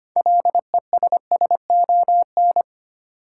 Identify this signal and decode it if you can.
Morse code, heard as a keyed tone, about 25 wpm, 700 Hz. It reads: LESSON
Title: Sentences from Top 1200 Words - New Word Drill - 25wpm